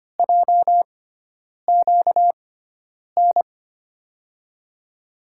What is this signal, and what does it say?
Morse code, heard as a keyed tone, about 25 wpm, spaced as for 9 wpm, 700 Hz. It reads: JQN
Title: USA Call Signs - 6 Characters - 6x Character Spacing - 25wpm